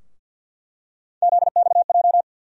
Morse code, heard as a keyed tone, about 47 wpm, 700 Hz. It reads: ZXJ
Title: USA Call Sign Postfixes - 3 Letters 50wpm